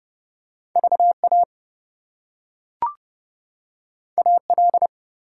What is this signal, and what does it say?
Morse code, heard as a keyed tone, about 32 wpm, 700 Hz. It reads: VA AL
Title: US States and Canadian Provinces 30wpm